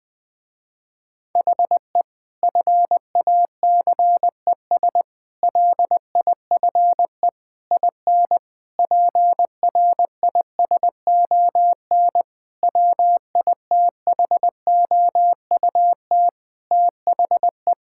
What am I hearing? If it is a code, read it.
Morse code, heard as a keyed tone, about 20 wpm, 700 Hz. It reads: HE FACES LIFE IN PRISON WITHOUT THE
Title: Sentences with 8-Word Phrases - Spoken Beginning - 20wpm